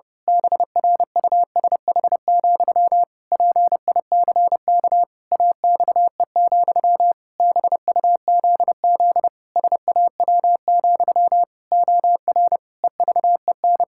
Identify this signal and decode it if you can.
Morse code, heard as a keyed tone, about 30 wpm, 700 Hz. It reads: BRUSH, PICK AXE, BUZZ SAW, OR EVEN
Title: Pangram Challenge v2 - 30wpm